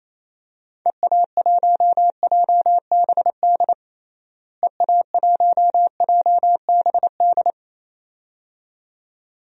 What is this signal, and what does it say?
Morse code, heard as a keyed tone, about 28 wpm, 700 Hz. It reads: EA1JBD EA1JBD
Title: DXCC-100 Call Signs - 6 Characters - Repeated 2x - 28wpm